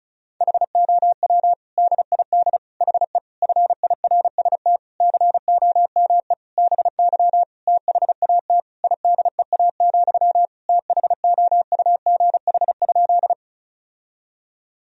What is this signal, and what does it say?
Morse code, heard as a keyed tone, about 36 wpm, 700 Hz. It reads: HOW DID HE FIRST COME BY THAT IDEA, THOUGH?
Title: Sentences from Top 1000 Words - Review Part 2 - 35wpm